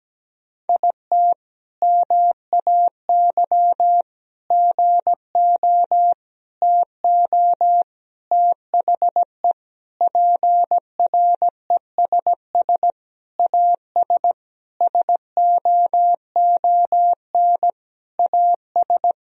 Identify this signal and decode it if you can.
Morse code, heard as a keyed tone, about 17 wpm, 700 Hz. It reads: IT MAY GO TO THE PRESS AS SOON AS
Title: Sentences from Top 1000 Words - Review Part 1 - 17wpm